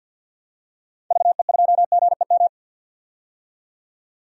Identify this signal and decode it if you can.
Morse code, heard as a keyed tone, about 52 wpm, 700 Hz. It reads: VE2GEM
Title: DXCC-100 Call Signs - 6 Characters - 50wpm